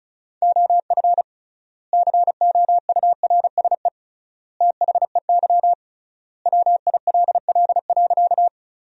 Morse code, heard as a keyed tone, about 34 wpm, 700 Hz. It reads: OF COURSE THEY WILL.
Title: 3x Word-Spacing - Sentences from Top 500 Words - 35wpm